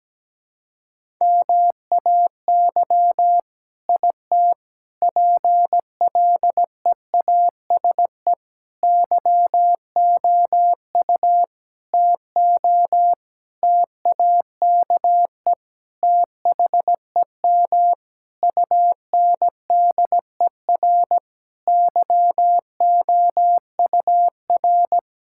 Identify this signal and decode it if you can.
Morse code, heard as a keyed tone, about 17 wpm, 700 Hz. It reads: MAY IT PLEASE YOU TO TAKE THEM UNDER YOUR
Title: Sentences from Top 1600 Words 17wpm